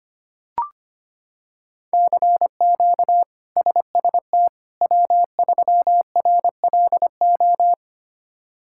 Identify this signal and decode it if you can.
Morse code, heard as a keyed tone, about 25 wpm, 700 Hz. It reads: CQ SST W3RLO
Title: K1USN SST QSO Practice - 25wpm